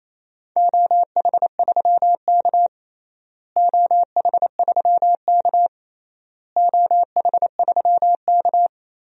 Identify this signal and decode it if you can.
Morse code, heard as a keyed tone, about 28 wpm, 700 Hz. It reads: OH3K OH3K OH3K
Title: DXCC-100 Call Signs - 4 Characters - Repeated 3x - 28wpm